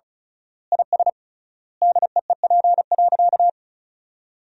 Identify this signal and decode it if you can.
Morse code, heard as a keyed tone, about 36 wpm, 700 Hz. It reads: IS DEEP.
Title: 3x Word-Spacing - Sentences from Top 500 Words - 35wpm